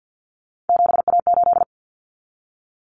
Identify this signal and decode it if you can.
Morse code, heard as a keyed tone, about 50 wpm, 700 Hz. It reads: 7A8
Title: Common Cut Numbers - 3-Digit 50wpm